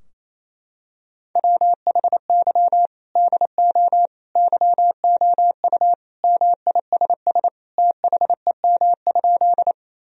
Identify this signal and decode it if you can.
Morse code, heard as a keyed tone, about 28 wpm, 700 Hz. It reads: WHY DO YOU MISS THEM?
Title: Sentences from Top 1000 Words - Review Part 3 - 28wpm